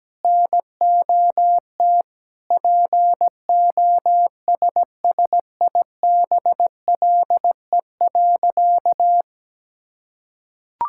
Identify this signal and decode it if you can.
Morse code, heard as a keyed tone, about 17 wpm, 700 Hz. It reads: NOT POSSIBLE.
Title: Sentences from Top 1000 Words - Review Part 6 - 17wpm